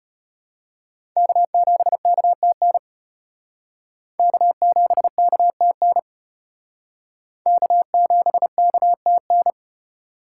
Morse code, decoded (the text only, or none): K7KTN K7KTN K7KTN